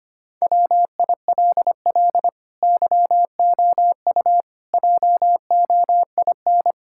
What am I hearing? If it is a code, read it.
Morse code, heard as a keyed tone, about 25 wpm, 700 Hz. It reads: WILL YOU JOIN
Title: Sentences from Top 1600 Words 25wpm